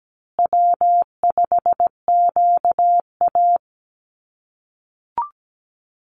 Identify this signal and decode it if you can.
Morse code, heard as a keyed tone, about 17 wpm, 700 Hz. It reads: W5QA
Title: USA Call Signs Repeated 2x 17wpm